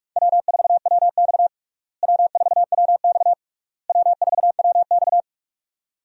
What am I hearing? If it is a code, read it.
Morse code, heard as a keyed tone, about 46 wpm, 700 Hz. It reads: W4WX W4WX W4WX
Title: USA Call Signs - 4 Characters - Repeated 3x - 45wpm